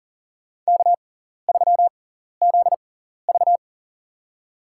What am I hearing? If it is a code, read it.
Morse code, heard as a keyed tone, about 40 wpm, spaced as for 14 wpm, 700 Hz. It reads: K3ZV
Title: USA Call Signs - 4 Characters - 6x Character Spacing - 40wpm